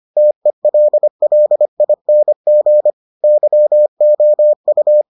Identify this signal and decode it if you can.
Morse code, heard as a keyed tone, about 25 wpm, 600 Hz. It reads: TELLING YOU